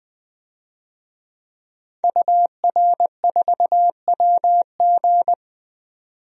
Morse code, heard as a keyed tone, about 20 wpm, 700 Hz. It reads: UR4WG
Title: DXCC-100 Call Signs - 5 Characters - 20wpm